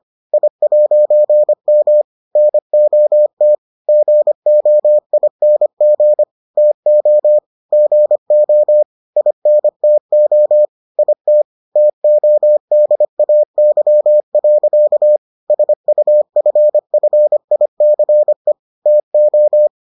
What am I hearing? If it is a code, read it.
Morse code, heard as a keyed tone, about 25 wpm, 600 Hz. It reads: I'M NOT GOING TO GO INTO IT TODAY. SUFFICE TO